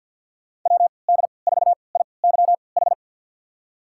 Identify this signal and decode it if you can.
Morse code, heard as a keyed tone, about 52 wpm, spaced as for 29 wpm, 700 Hz. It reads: WD4IYH
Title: USA Call Signs - 6 Characters - 3x Character Spacing - 50wpm